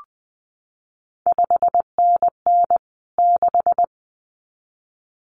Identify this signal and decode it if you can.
Morse code, heard as a keyed tone, about 20 wpm, 700 Hz. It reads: 5NN 6
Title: CQ WW DX Contest Exchange Encore - 20wpm